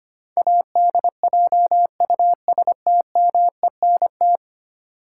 Morse code, heard as a keyed tone, about 25 wpm, 700 Hz. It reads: ADJUSTMENT
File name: Ten Letter Words - 25wpm